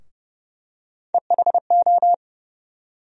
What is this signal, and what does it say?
Morse code, heard as a keyed tone, about 30 wpm, 700 Hz. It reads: EHO
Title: USA Call Sign Postfixes - 3 Letters 30wpm